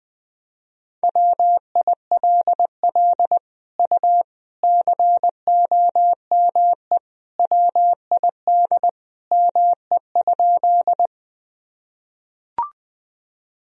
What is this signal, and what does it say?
Morse code, heard as a keyed tone, about 20 wpm, 700 Hz. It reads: WILL U COME WID ME?
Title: Sentences from Top 200 Words and Abbreviations 20wpm